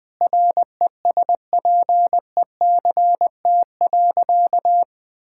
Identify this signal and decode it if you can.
Morse code, heard as a keyed tone, about 20 wpm, 700 Hz. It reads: RESPECT.
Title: Sentences with 4-Word Phrases - 20wpm